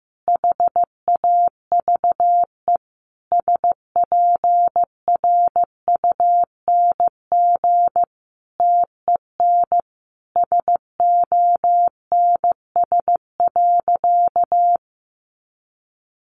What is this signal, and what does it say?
Morse code, heard as a keyed tone, about 15 wpm, 700 Hz. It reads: HAVE SPRUNG TEN SONS.
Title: A Sound Of Thunder 15wpm